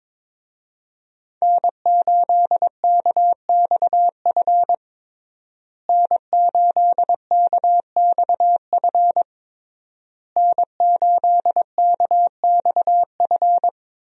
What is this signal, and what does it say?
Morse code, heard as a keyed tone, about 22 wpm, 700 Hz. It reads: N8KXF N8KXF N8KXF
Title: USA Call Signs - 5 Characters - Repeated 3x - 22wpm